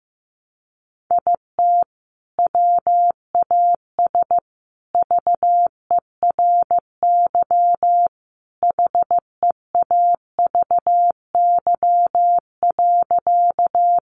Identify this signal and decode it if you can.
Morse code, heard as a keyed tone, about 15 wpm, 700 Hz. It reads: IT WAS VERY HEAVY.
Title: Sentences from Top 1100 Words 15wpm